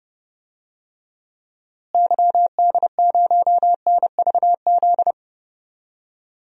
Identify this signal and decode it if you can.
Morse code, heard as a keyed tone, about 32 wpm, 700 Hz. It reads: YD0NVZ
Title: DXCC-100 Call Signs - 6 Characters - Repeated 3x - 30wpm